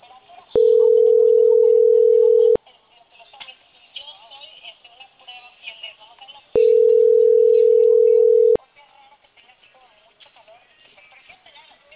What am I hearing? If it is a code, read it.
Ringback tone (North America), 440 + 480 Hz.